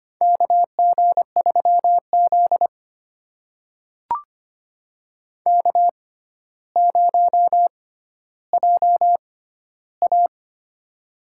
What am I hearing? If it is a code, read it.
Morse code, heard as a keyed tone, about 25 wpm, spaced as for 13 wpm, 700 Hz. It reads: KG3Z K0JA
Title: USA Call Signs - 4 Characters - 6x Character Spacing - 25wpm